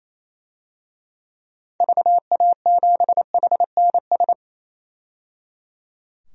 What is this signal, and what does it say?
Morse code, heard as a keyed tone, about 28 wpm, 700 Hz. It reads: VA7HNS
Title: DXCC-100 Call Signs - 6 Characters - 28wpm